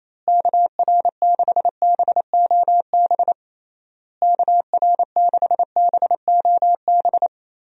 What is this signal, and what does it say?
Morse code, heard as a keyed tone, about 28 wpm, 700 Hz. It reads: KR6BOB KR6BOB
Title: USA Call Signs - 6 Characters - Repeated 3x - 28wpm